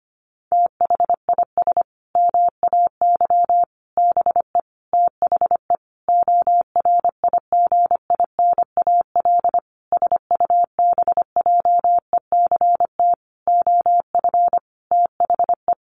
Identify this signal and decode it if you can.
Morse code, heard as a keyed tone, about 25 wpm, 700 Hz. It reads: THIS MAY BE THE ORIGINAL SUBJECT OF THE
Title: Sentences from Top 900 Words 25wpm